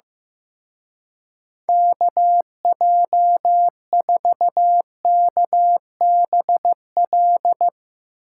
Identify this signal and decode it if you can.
Morse code, heard as a keyed tone, about 16 wpm, 700 Hz. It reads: KJ4KBL
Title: Speed Racing - USA 2x3 Call Signs 15wpm